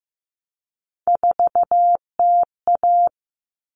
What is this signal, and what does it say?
Morse code, heard as a keyed tone, about 15 wpm, 700 Hz. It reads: 4TA
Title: All Cut Numbers - 3-Digit 15wpm